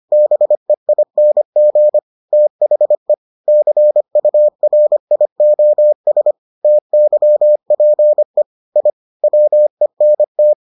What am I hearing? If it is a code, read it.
Morse code, heard as a keyed tone, about 25 wpm, 600 Hz. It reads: BEING THE CURIOS TYPE I WENT